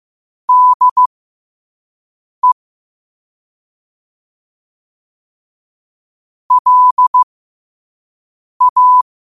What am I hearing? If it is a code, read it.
Morse code, heard as a keyed tone, about 15 wpm, spaced as for 5 wpm, 1000 Hz. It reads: DE LA
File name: qotdfiles / 5wpm